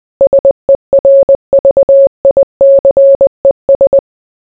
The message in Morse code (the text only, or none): SERVICES